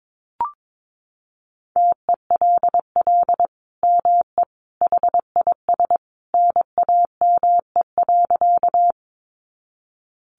Morse code, heard as a keyed tone, about 22 wpm, 700 Hz. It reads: TELL ME HIS NAME.
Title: Sentences from Top 200 Words 22wpm